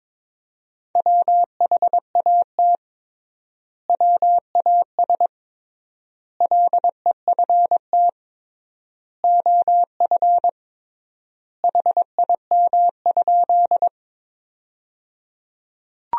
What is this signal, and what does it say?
Morse code, heard as a keyed tone, about 21 wpm, 700 Hz. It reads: WHAT WAS LEFT OF HIM?
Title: 3x Word-Spacing - Sentences from Top 300 Words - 22wpm